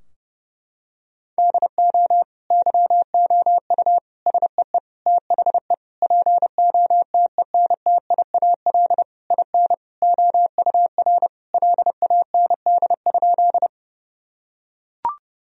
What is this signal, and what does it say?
Morse code, heard as a keyed tone, about 32 wpm, 700 Hz. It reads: DO YOU SEE THE POTENTIAL IN OUR LAND?
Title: Sentences from Top 1000 Words - Review Part 4 - 30wpm